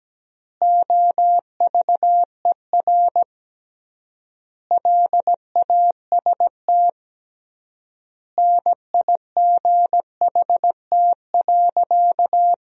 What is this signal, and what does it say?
Morse code, heard as a keyed tone, about 17 wpm, 700 Hz. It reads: OVER LAST NIGHT.
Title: 3x Word-Spacing - Sentences from Top 300 Words - 17wpm